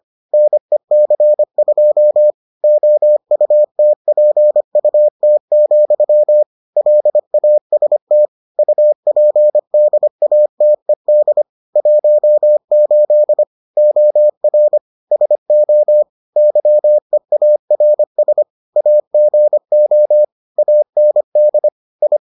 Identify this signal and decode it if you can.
Morse code, heard as a keyed tone, about 25 wpm, 600 Hz. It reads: NEC2 OUTPUT, LAST UPDATED 18 OR SO YEARS AGO AND I